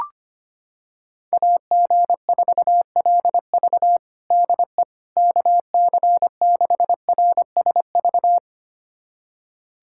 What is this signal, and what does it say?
Morse code, heard as a keyed tone, about 25 wpm, 700 Hz. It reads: AG4LV DE KC6RSV
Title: USA Callsign DE Callsign 25wpm